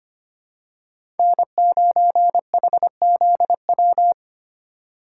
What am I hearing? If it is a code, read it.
Morse code, heard as a keyed tone, about 25 wpm, 700 Hz. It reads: N9HZW
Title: USA Call Signs - 5 Characters - 4x Character Spacing - 25wpm